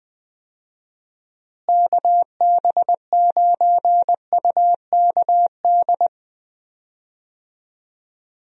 Morse code, heard as a keyed tone, about 20 wpm, 700 Hz. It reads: KB9UKD
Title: USA Call Signs - 6 Characters - Repeated 3x - 20wpm